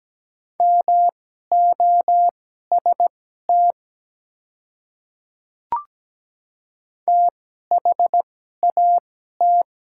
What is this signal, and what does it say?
Morse code, heard as a keyed tone, about 17 wpm, spaced as for 12 wpm, 700 Hz. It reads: MOST THAT
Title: 2x Character Spacing - Top 100 Words - 17wpm